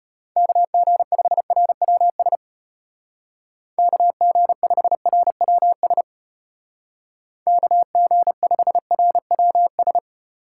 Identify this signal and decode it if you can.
Morse code, heard as a keyed tone, about 34 wpm, 700 Hz. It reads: KG5RWS KG5RWS KG5RWS